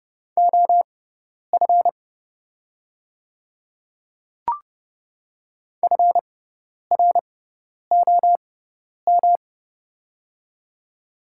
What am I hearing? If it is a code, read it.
Morse code, heard as a keyed tone, about 30 wpm, spaced as for 10 wpm, 700 Hz. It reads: OF FROM
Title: 6x Character Spacing - Top 100 Words - 30wpm